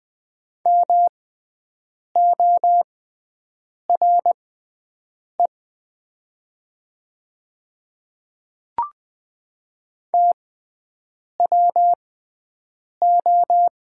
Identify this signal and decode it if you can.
Morse code, heard as a keyed tone, about 20 wpm, spaced as for 7 wpm, 700 Hz. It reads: MORE TWO
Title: 6x Character Spacing - Top 100 Words - 20wpm